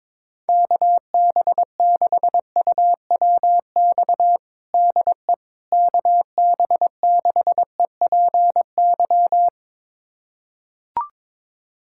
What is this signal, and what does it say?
Morse code, heard as a keyed tone, about 21 wpm, 700 Hz. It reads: KB6UWX DE KB6EPY
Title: USA Callsign DE Callsign 22wpm